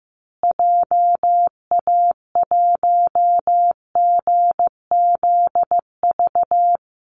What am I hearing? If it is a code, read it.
Morse code, heard as a keyed tone, about 15 wpm, 700 Hz. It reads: JA1GZV
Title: Call Signs Repeated 2x - 15wpm